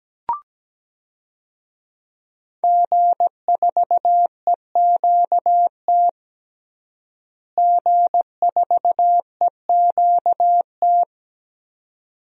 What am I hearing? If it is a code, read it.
Morse code, heard as a keyed tone, about 17 wpm, 700 Hz. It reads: G4EQT G4EQT
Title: DXCC-100 Call Signs - 5 Characters - Repeated 2x - 17wpm